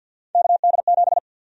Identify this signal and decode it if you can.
Morse code, heard as a keyed tone, about 51 wpm, 700 Hz. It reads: KD7
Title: USA Call Sign Prefixes - 50wpm